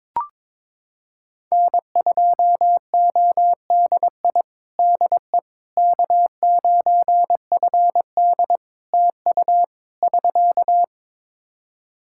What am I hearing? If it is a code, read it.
Morse code, heard as a keyed tone, about 22 wpm, 700 Hz. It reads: N2ODI DE K9FD TU <SK>
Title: Practice QSO - Exchange 4 - Ending - 22wpm